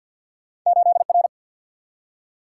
Morse code, heard as a keyed tone, about 48 wpm, 700 Hz. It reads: 8 R
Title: Single-Letter Single-Number 50wpm